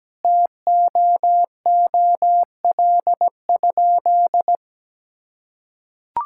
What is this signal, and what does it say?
Morse code, heard as a keyed tone, about 17 wpm, 700 Hz. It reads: TOOL?
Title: Sentences from Top 1000 Words - Review Part 6 - 17wpm